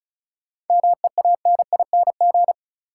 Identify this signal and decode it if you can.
Morse code, heard as a keyed tone, about 36 wpm, 700 Hz. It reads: MEANING
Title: Sentences from Top 900 Words - New Word Drill - 35wpm